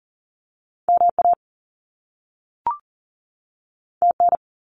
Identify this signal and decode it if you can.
Morse code, heard as a keyed tone, about 40 wpm, 700 Hz. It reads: MA TN
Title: US States 40wpm